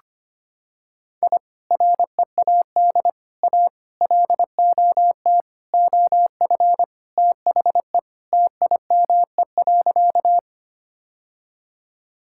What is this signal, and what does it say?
Morse code, heard as a keyed tone, about 25 wpm, 700 Hz. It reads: I READ A LOT OF THE TIME.
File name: practice / 25wpm / Sentences with 4-Word Phrases - 25wpm